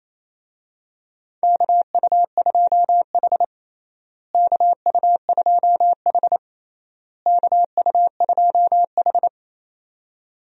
Morse code, heard as a keyed tone, about 27 wpm, 700 Hz. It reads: KU2H KU2H KU2H
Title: USA Call Signs - 4 Characters - Repeated 3x - 28wpm